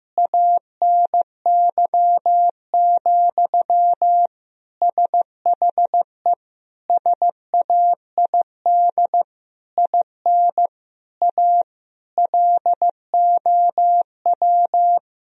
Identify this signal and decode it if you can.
Morse code, heard as a keyed tone, about 15 wpm, 700 Hz. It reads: ANY, SHE SAID IN A LOW
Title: Sentences from Top 1000 Words - Review Part 1 - 15wpm